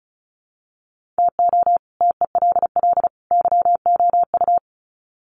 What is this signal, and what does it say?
Morse code, heard as a keyed tone, about 34 wpm, 700 Hz. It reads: TO TELL YOU
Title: Sets of 3 Words 35wpm